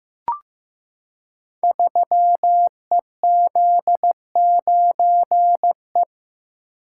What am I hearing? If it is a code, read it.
Morse code, heard as a keyed tone, about 15 wpm, 700 Hz. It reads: 3EZ9E
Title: Code Groups - 5 Letter:Numbers - 15wpm